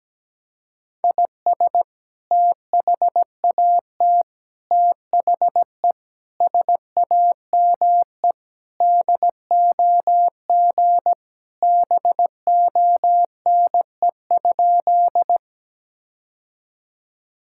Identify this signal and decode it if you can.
Morse code, heard as a keyed tone, about 17 wpm, 700 Hz. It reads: IS THAT THE SAME DOG BONE?
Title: Sentences from Top 1600 Words 17wpm